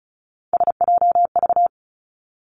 Morse code, heard as a keyed tone, about 34 wpm, 700 Hz. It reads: SJV
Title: ARRL Sections 35wpm